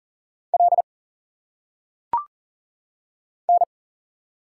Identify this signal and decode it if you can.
Morse code, heard as a keyed tone, about 40 wpm, 700 Hz. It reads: L N
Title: Single Letters - Phonetic Pronunciation 40wpm